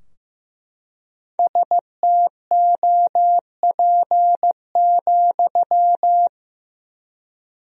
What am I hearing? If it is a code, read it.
Morse code, heard as a keyed tone, about 15 wpm, 700 Hz. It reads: STOP,
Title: 4x Word-Spacing - Sentences from Top 300 Words - 15wpm